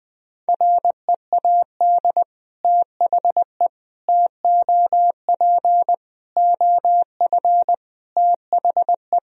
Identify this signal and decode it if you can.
Morse code, heard as a keyed tone, about 20 wpm, 700 Hz. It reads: READ THE TOP OF THE
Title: Sentences from Top 1000 Words - Review Part 2 - 20wpm